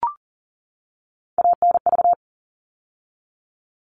Morse code, heard as a keyed tone, about 40 wpm, 700 Hz. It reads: ANV
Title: All Cut Numbers - 3-Digit 40wpm